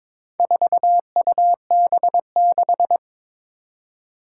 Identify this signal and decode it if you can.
Morse code, heard as a keyed tone, about 22 wpm, 700 Hz. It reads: 4UB6
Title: Four Challenging Letter-Number Combinations 22wpm